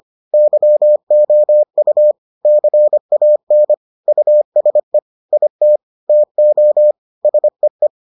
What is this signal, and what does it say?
Morse code, heard as a keyed tone, about 25 wpm, 600 Hz. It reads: YOU CAN USE IT TO SEE